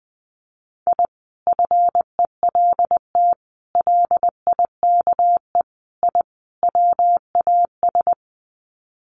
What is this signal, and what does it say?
Morse code, heard as a keyed tone, about 19 wpm, 700 Hz. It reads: I FELT LIKE I WAS
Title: Sets of 5 Words Encore 20wpm